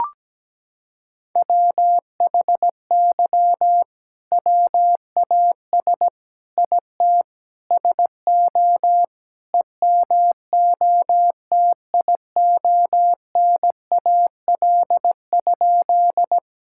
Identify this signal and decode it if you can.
Morse code, heard as a keyed tone, about 17 wpm, 700 Hz. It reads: WHY WAS IT SO EMOTIONAL?
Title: Sentences from Top 1000 Words - Review Part 5 - 17wpm